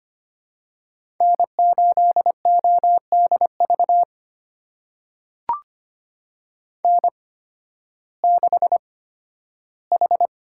Morse code, heard as a keyed tone, about 25 wpm, spaced as for 13 wpm, 700 Hz. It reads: N8ODV N6H
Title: USA Call Signs - 5 Characters - 8x Character Spacing - 25wpm